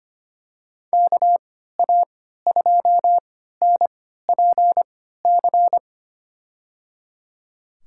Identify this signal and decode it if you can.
Morse code, heard as a keyed tone, about 25 wpm, spaced as for 14 wpm, 700 Hz. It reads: KA2NPC
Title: USA Call Signs - 6 Characters - 3x Character Spacing - 25wpm